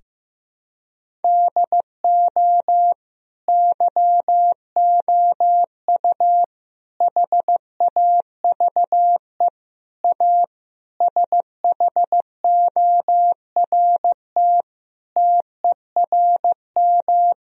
Morse code, heard as a keyed tone, about 16 wpm, 700 Hz. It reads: DO YOU HAVE A SHORT TERM
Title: Sentences from Top 1000 Words - Review Part 4 - 15wpm